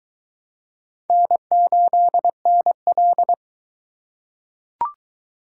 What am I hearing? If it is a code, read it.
Morse code, heard as a keyed tone, about 22 wpm, 700 Hz. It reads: N8NL